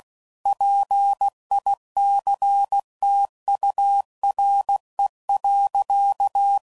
Morse code, heard as a keyed tone, about 16 wpm, 800 Hz. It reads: PICTURE.
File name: CW Code Groups @15WPM / Technical Subjects / AND Gate